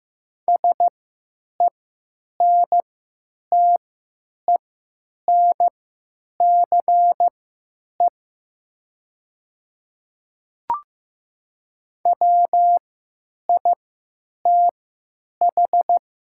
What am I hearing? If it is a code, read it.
Morse code, heard as a keyed tone, about 15 wpm, spaced as for 9 wpm, 700 Hz. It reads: SENTENCE WITH
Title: 3x Character Spacing - Top 500 Words - 15wpm